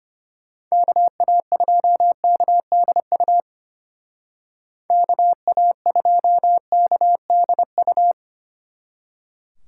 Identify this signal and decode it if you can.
Morse code, heard as a keyed tone, about 31 wpm, 700 Hz. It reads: KA2KDU KA2KDU
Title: Speed Racing - USA 2x3 Call Signs 25wpm